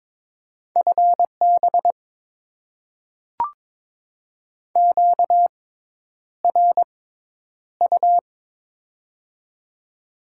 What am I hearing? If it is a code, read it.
Morse code, heard as a keyed tone, about 22 wpm, spaced as for 10 wpm, 700 Hz. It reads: FB QRU
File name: Top 90 QSO Elements - 6x Character Spacing - 22wpm